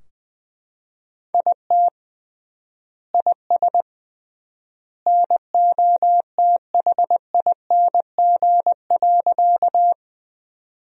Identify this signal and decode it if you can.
Morse code, heard as a keyed tone, about 20 wpm, 700 Hz. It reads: IT IS NOTHING.
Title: 3x Word-Spacing - Sentences from Top 500 Words - 20wpm